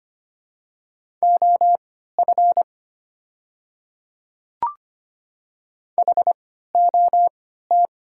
Morse code, heard as a keyed tone, about 25 wpm, spaced as for 14 wpm, 700 Hz. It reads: OF HOT
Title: 3x Character Spacing - Top 200 Words - 25wpm